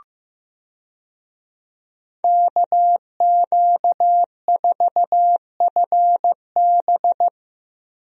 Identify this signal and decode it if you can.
Morse code, heard as a keyed tone, about 15 wpm, 700 Hz. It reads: KQ4FB
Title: USA Call Signs - 5 Characters - 15wpm